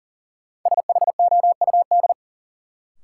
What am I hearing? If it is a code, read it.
Morse code, heard as a keyed tone, about 40 wpm, 700 Hz. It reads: SHOUD
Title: Sentences from Top 1000 Words - Review Part 4 - New Word Drill - 40wpm